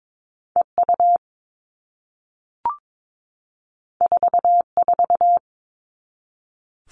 Morse code, heard as a keyed tone, about 22 wpm, 700 Hz. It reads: EU 44
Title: All Cut Numbers - 2-Digit 22wpm